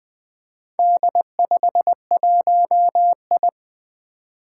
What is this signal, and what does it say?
Morse code, heard as a keyed tone, about 21 wpm, 700 Hz. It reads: D51I
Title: Four Challenging Letter-Number Combinations 20wpm